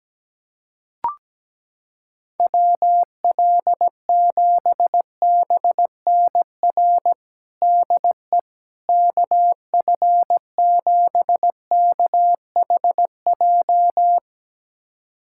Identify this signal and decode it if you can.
Morse code, heard as a keyed tone, about 17 wpm, 700 Hz. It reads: WL7BNR DE KF7KHJ
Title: USA Callsign DE Callsign - 6 Characters - 17wpm